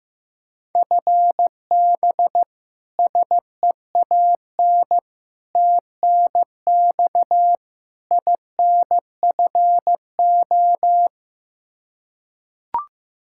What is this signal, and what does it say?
Morse code, heard as a keyed tone, about 15 wpm, 700 Hz. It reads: FB SEAN TNX INFO
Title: Practice QSO - Exchange 3 - More Info - 15wpm